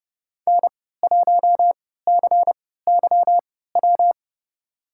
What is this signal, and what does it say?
Morse code, heard as a keyed tone, about 30 wpm, spaced as for 17 wpm, 700 Hz. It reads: N1CYW
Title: USA Call Signs - 5 Characters - 3x Character Spacing - 30wpm